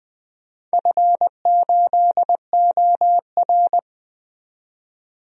Morse code, heard as a keyed tone, about 20 wpm, 700 Hz. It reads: F8OR
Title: DXCC-100 Call Signs - 4 Characters - 20wpm